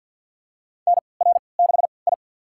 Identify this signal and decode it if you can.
Morse code, heard as a keyed tone, about 46 wpm, spaced as for 27 wpm, 700 Hz. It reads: NR6I
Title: USA Call Signs - 4 Characters - 3x Character Spacing - 50wpm